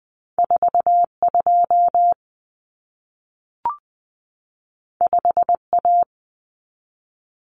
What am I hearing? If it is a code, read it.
Morse code, heard as a keyed tone, about 20 wpm, 700 Hz. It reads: 42 5A
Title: Common Cut Numbers - 2-Digit 20wpm